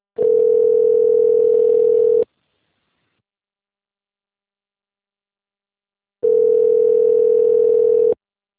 Ringback tone (North America), 440 + 480 Hz.
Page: Guarapuava - PR